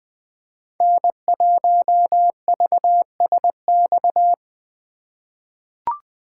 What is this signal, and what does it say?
Morse code, heard as a keyed tone, about 20 wpm, 700 Hz. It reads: N1VSX
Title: USA Call Signs - 5 Characters - 4x Character Spacing - 20wpm